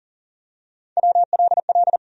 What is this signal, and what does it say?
Morse code, heard as a keyed tone, about 40 wpm, 700 Hz. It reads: WLL
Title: USA Call Sign Postfixes - 3 Letters 40wpm